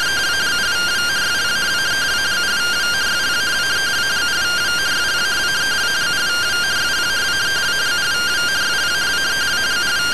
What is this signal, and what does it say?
Two-tone teleprinter signal, about 170 Hz shift.